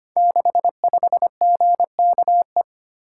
Morse code, heard as a keyed tone, about 25 wpm, 700 Hz. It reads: 65GKE
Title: Code Groups - 5 Letter:Numbers - 25wpm